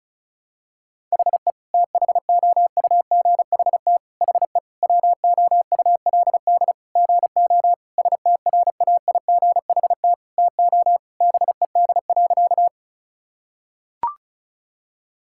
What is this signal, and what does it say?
Morse code, heard as a keyed tone, about 36 wpm, 700 Hz. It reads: HE THOUGHT HE WOULD GO STRAIGHT TO BED.
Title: Sentences from Top 1000 Words - Review Part 6 - 35wpm